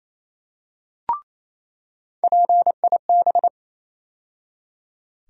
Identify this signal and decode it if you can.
Morse code, heard as a keyed tone, about 27 wpm, 700 Hz. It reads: PIB
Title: USA Call Sign Postfixes - 3 Letters 28wpm